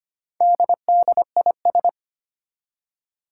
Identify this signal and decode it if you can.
Morse code, heard as a keyed tone, about 25 wpm, 700 Hz. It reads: DDIS
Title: Four Challenging Letter-Number Combinations 25wpm